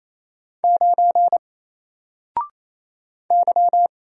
Morse code, heard as a keyed tone, about 27 wpm, 700 Hz. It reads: 9 Y
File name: Single-Letter-Number-28wpm